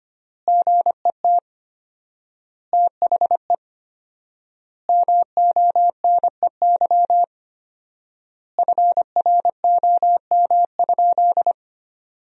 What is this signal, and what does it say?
Morse code, heard as a keyed tone, about 25 wpm, 700 Hz. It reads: GET THE MONEY FROM?
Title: 4x Word-Spacing - Sentences from Top 400 Words - 25wpm